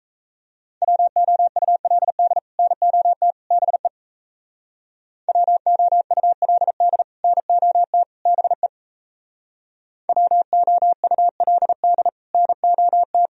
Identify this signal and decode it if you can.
Morse code, heard as a keyed tone, about 34 wpm, 700 Hz. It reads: WOULD NOT BE WOULD NOT BE WOULD NOT
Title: Speed-Racing - Sets of 3 Words 28wpm